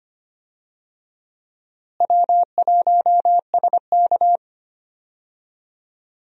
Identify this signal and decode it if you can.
Morse code, heard as a keyed tone, about 25 wpm, 700 Hz. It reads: W1SK
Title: USA Call Signs - 4 Characters - Repeated 3x - 25wpm